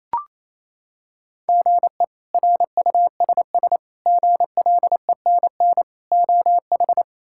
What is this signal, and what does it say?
Morse code, heard as a keyed tone, about 28 wpm, 700 Hz. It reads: GE RUSS GLENN OH
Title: K1USN SST QSO Practice - 28wpm